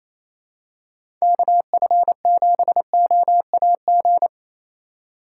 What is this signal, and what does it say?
Morse code, heard as a keyed tone, about 28 wpm, 700 Hz. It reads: KF7OAG
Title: USA Call Signs - 6 Characters - 2x Character Spacing - 28wpm